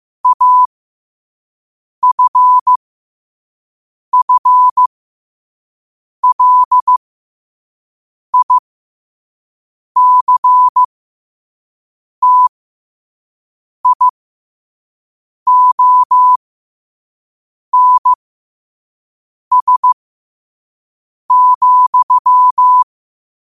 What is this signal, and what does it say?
Morse code, heard as a keyed tone, about 15 wpm, spaced as for 5 wpm, 1000 Hz. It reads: AFFLICTIONS,
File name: qotdfiles / 5wpm